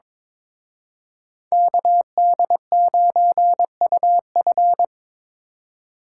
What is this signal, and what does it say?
Morse code, heard as a keyed tone, about 26 wpm, 700 Hz. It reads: KD9UF
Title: Speed Racing - USA 2x2 Call Signs 22wpm